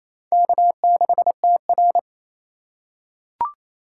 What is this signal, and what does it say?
Morse code, heard as a keyed tone, about 27 wpm, 700 Hz. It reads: K6TR
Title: USA Call Signs - 4 Characters - 4x Character Spacing - 28wpm